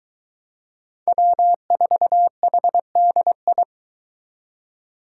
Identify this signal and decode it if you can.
Morse code, heard as a keyed tone, about 21 wpm, 700 Hz. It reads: W4HDI